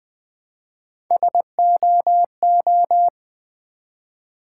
Morse code, heard as a keyed tone, about 20 wpm, 700 Hz. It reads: SOO
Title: Build A Word - 1 Letter At A Time - 3 to 5-Letter Words - 20wpm